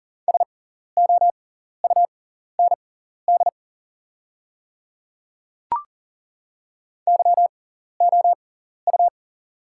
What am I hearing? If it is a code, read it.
Morse code, heard as a keyed tone, about 40 wpm, spaced as for 14 wpm, 690 Hz. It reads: SOUND YOU
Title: 6x Character Spacing - Top 100 Words - 40wpm